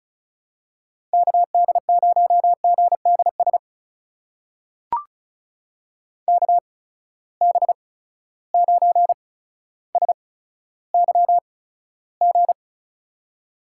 Morse code, heard as a keyed tone, about 34 wpm, spaced as for 15 wpm, 700 Hz. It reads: KD0GDS KB9SYG
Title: USA Call Signs - 6 Characters - 8x Character Spacing - 35wpm